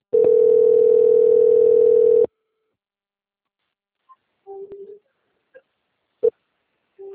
Ringback tone (North America), 440 + 480 Hz.